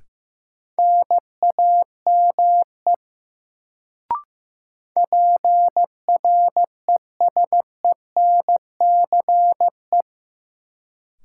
Morse code, heard as a keyed tone, about 16 wpm, 700 Hz. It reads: NAME PRESENCE
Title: Sets of 3 Words - New Word Drill - 15wpm